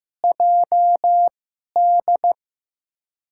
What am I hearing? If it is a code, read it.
Morse code, heard as a keyed tone, about 15 wpm, spaced as for 11 wpm, 700 Hz. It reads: JD
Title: USA Call Signs - 2x Character Spacing - 15wpm